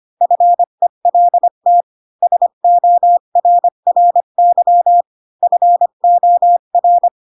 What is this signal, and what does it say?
Morse code, heard as a keyed tone, about 25 wpm, 700 Hz. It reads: FELT SORRY FOR
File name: Pinocchio / Pinocchio ch7 20 25 ik1hge